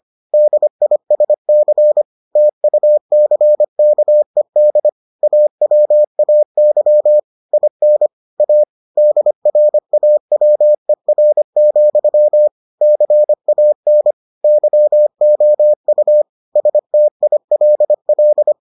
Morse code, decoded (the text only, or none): DISC TUCKED AWAY IN A DRAWER, CAN YOU STILL